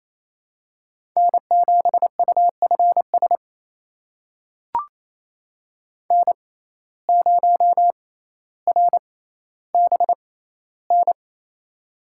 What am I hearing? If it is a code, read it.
Morse code, heard as a keyed tone, about 28 wpm, spaced as for 14 wpm, 700 Hz. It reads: N7UFS N0RBN
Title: USA Call Signs - 5 Characters - 6x Character Spacing - 28wpm